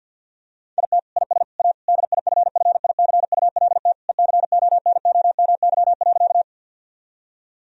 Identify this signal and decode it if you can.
Morse code, heard as a keyed tone, about 50 wpm, 700 Hz. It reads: IT IS A DIFFICULT ECONOMY.
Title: Sentences from Top 1000 Words - Review Part 3 - 50wpm